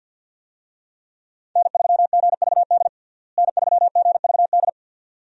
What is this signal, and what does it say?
Morse code, heard as a keyed tone, about 49 wpm, 700 Hz. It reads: N3GVD N3GVD
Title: USA Call Signs - 5 Characters - Repeated 2x - 50wpm